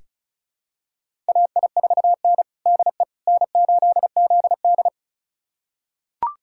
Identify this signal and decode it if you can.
Morse code, heard as a keyed tone, about 34 wpm, 700 Hz. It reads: AI4N DE N8ZD
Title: USA Callsign DE Callsign - 4 Characters - 35wpm